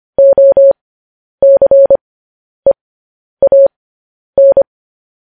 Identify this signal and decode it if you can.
Morse code, heard as a keyed tone, about 25 wpm, spaced as for 10 wpm, 550 Hz. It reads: OCEAN
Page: news-headlines-in-morse-code-at-10-wpm